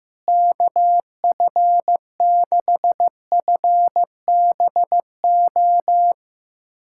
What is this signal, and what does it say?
Morse code, heard as a keyed tone, about 15 wpm, 700 Hz. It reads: KF6FBO
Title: USA Callsign DE Callsign 15wpm